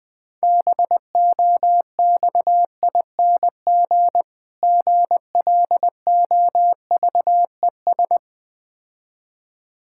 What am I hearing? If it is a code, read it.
Morse code, heard as a keyed tone, about 21 wpm, 690 Hz. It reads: BOXING GLOVES
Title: Difficult Words with Hints - 20wpm